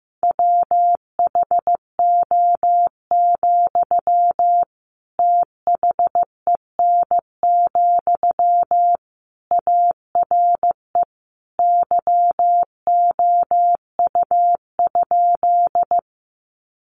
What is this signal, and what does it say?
Morse code, heard as a keyed tone, about 15 wpm, 700 Hz. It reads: WHO, THEN, ARE YOU?
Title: Sentences from Top 100 Words 15wpm